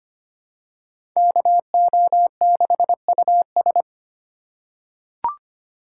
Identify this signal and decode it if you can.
Morse code, heard as a keyed tone, about 25 wpm, 700 Hz. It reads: KO6US
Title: USA Call Signs - 2x Character Spacing - 25wpm